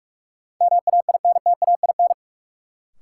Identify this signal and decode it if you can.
Morse code, heard as a keyed tone, about 42 wpm, 700 Hz. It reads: MAINTAIN
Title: Sentences from Top 1000 Words - Review Part 5 - New Word Drill - 45wpm